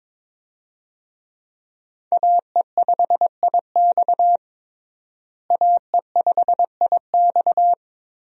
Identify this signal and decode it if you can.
Morse code, heard as a keyed tone, about 22 wpm, 700 Hz. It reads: AE5IX AE5IX
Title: USA Call Signs - 5 Characters - Repeated 3x - 22wpm